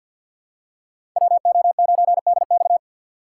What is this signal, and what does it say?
Morse code, heard as a keyed tone, about 53 wpm, 700 Hz. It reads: WO9DX
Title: USA Call Signs - 5 Characters - 6x Character Spacing - 50wpm